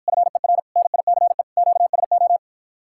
Morse code, heard as a keyed tone, about 53 wpm, 700 Hz. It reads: FER NICE QSO